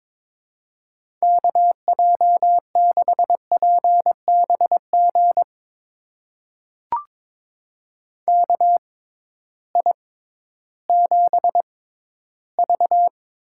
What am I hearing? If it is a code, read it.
Morse code, heard as a keyed tone, about 22 wpm, spaced as for 13 wpm, 700 Hz. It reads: KJ6PBG KI7V
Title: USA Call Signs - 6 Characters - 6x Character Spacing - 22wpm